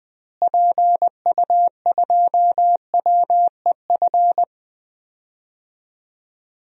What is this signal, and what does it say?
Morse code, heard as a keyed tone, about 20 wpm, 700 Hz. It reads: PU2WEF
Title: DXCC-100 Call Signs - 6 Characters - Repeated 3x - 20wpm